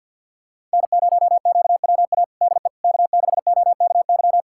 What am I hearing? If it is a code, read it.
Morse code, heard as a keyed tone, about 47 wpm, 690 Hz. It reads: N0QWA DE K6OKY
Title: USA Callsign DE Callsign - 5 Characters - 50wpm